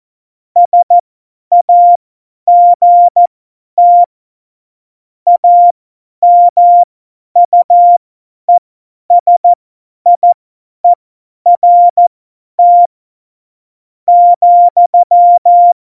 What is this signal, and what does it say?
Morse code, heard as a keyed tone, about 14 wpm, spaced as for 9 wpm, 700 Hz. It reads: SAGT AMUESIERT,
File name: Morsekurs / Witze / Witz 20. Oktober 2020, 50 F70